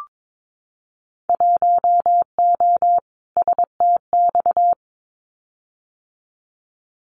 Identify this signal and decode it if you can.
Morse code, heard as a keyed tone, about 22 wpm, 700 Hz. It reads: 1O STX
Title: Winter Field Day Exchanges 22wpm